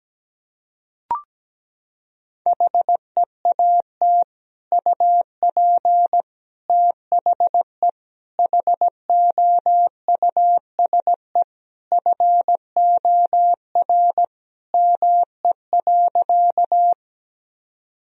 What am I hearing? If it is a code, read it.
Morse code, heard as a keyed tone, about 17 wpm, 700 Hz. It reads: HEAT UP THE HOUSE FOR ME.
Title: Sentences from Top 1000 Words - Review Part 3 - 17wpm